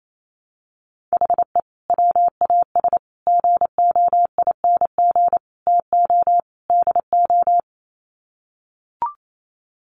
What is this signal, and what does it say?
Morse code, heard as a keyed tone, about 28 wpm, 700 Hz. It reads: HE WAS GOING TO DO
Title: Sets of 5 Words Encore2 28wpm